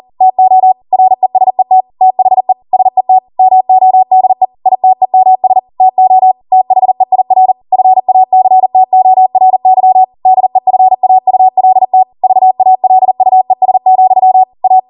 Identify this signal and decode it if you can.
Morse code, heard as a keyed tone, about 40 wpm, 750 Hz. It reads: TO RESET THE SET MODE ITEMS TO THEIR FACTORY DEFAULT VALUES, A
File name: Morse / 40WPM